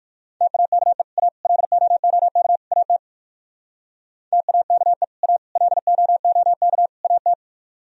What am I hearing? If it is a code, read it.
Morse code, heard as a keyed tone, about 58 wpm, 700 Hz. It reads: TAKE A LOOK AT TAKE A LOOK AT